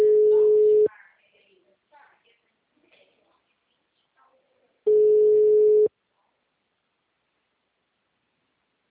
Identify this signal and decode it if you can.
Ringback tone (Europe, 425 Hz), with speech.
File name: out- -2022- - - .77594